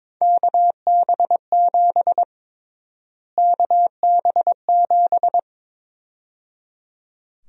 Morse code, heard as a keyed tone, about 22 wpm, 700 Hz. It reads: KB7 KB7
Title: USA Call Sign Prefixes - Repeated 2x - 22wpm